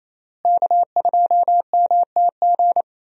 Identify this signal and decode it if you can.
Morse code, heard as a keyed tone, about 27 wpm, 700 Hz. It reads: K2MTG
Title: USA Call Signs - 5 Characters - Repeated 3x - 28wpm